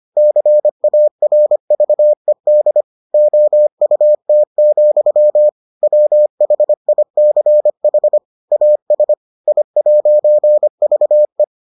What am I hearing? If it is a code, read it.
Morse code, heard as a keyed tone, about 25 wpm, 600 Hz. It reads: CARVED OUT, WHICH AS I'VE